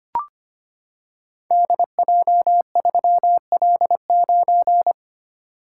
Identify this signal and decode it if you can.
Morse code, heard as a keyed tone, about 25 wpm, 690 Hz. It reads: DJ3L9
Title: Code Groups - 5 Letter:Numbers - 25wpm